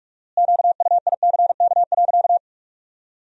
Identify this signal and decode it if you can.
Morse code, heard as a keyed tone, about 46 wpm, 700 Hz. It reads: QUICK.
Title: 4x Word-Spacing - Sentences from Top 500 Words - 45wpm